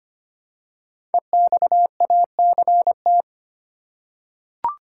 Morse code, heard as a keyed tone, about 25 wpm, 700 Hz. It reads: EXACT
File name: 5 Letter Words - First 3 Letters Given - 25wpm